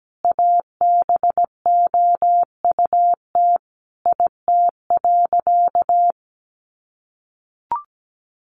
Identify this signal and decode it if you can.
Morse code, heard as a keyed tone, about 17 wpm, 700 Hz. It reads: ABOUT IT.
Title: Sentences from Top 1100 Words 17wpm